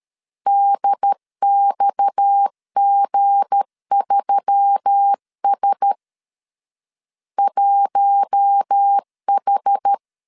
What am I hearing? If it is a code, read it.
Morse code, heard as a keyed tone, about 13 wpm, 800 Hz. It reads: DXG3S 1H
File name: CW / DET / 1 9 1500C S C 12wpm 3811